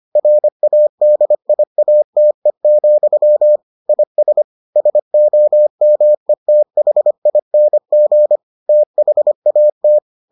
Morse code, heard as a keyed tone, about 25 wpm, 600 Hz. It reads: RADIATE, IS SOMETHING THAT